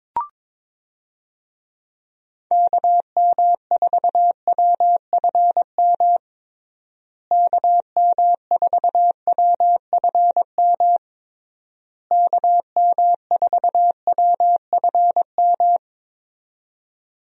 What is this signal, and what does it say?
Morse code, heard as a keyed tone, about 22 wpm, 700 Hz. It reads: KM4WFM KM4WFM KM4WFM
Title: USA Call Signs - 6 Characters - Repeated 3x - 22wpm